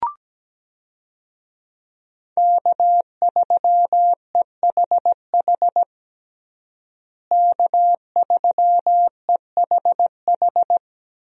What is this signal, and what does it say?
Morse code, heard as a keyed tone, about 17 wpm, 700 Hz. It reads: K3EHH K3EHH
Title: USA Call Signs - 5 Characters - Repeated 3x - 17wpm